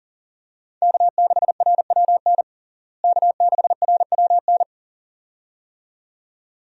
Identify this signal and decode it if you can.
Morse code, heard as a keyed tone, about 37 wpm, 700 Hz. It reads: K6RWN K6RWN
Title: USA Call Signs - 5 Characters - Repeated 2x - 40wpm